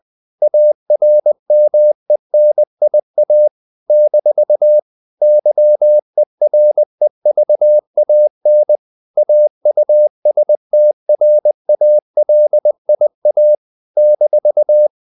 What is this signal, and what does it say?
Morse code, heard as a keyed tone, about 20 wpm, 600 Hz. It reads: ARMENIA - YEREVAN AUSTRALIA -
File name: CW / Practice / NationalCapitals20